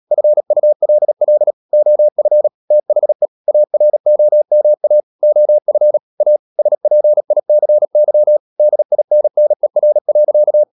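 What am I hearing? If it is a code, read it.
Morse code, heard as a keyed tone, about 37 wpm, 600 Hz. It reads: FULL OF THE AROMA OF A SPICY DINNER.
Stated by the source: Morse code practice at 35–39 wpm sending English text of Ken Follett's novel Never, from CW med Gnister.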